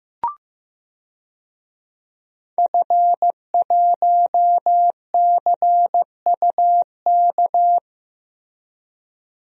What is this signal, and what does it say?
Morse code, heard as a keyed tone, about 15 wpm, 700 Hz. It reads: F1CUK
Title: DXCC-100 Call Signs - 5 Characters - Repeated 3x - 15wpm